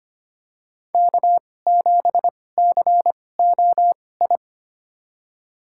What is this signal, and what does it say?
Morse code, heard as a keyed tone, about 25 wpm, spaced as for 18 wpm, 700 Hz. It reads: K7COI
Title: USA Call Signs - 5 Characters - 2x Character Spacing - 25wpm